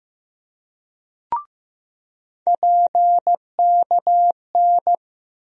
Morse code, heard as a keyed tone, about 15 wpm, 700 Hz. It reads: PKN
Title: USA Call Sign Postfixes - 3 Letters 15wpm